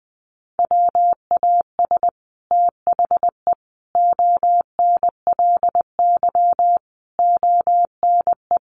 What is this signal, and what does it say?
Morse code, heard as a keyed tone, about 20 wpm, 700 Hz. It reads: WAS THE ONLY ONE
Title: Sets of 5 Words Encore2 20wpm